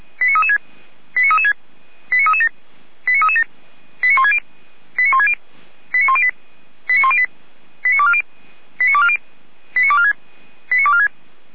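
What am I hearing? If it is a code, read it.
Five-tone selective call (ZVEI), heard as quick runs of five tones, tones of about 70 ms. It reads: 89336 89336 89337 89337 89160 89160 89108 89108 89240 89240 89246 89246
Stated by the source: ZVEI five-tone selective call received with an SDR.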